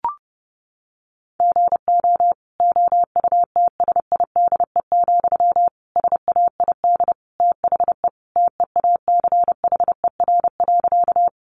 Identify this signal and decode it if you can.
Morse code, heard as a keyed tone, about 28 wpm, 700 Hz. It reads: GO OUTSIDE, SAID THE TEACHER.
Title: Sentences from Top 900 Words 30wpm